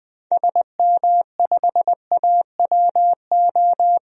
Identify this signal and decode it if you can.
Morse code, heard as a keyed tone, about 20 wpm, 700 Hz. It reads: SM5AWO
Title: DXCC-100 Call Signs - 6 Characters - 20wpm